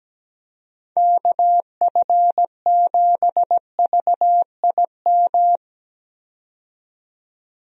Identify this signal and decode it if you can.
Morse code, heard as a keyed tone, about 17 wpm, 700 Hz. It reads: KF7VIM
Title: USA Call Signs - 6 Characters - 17wpm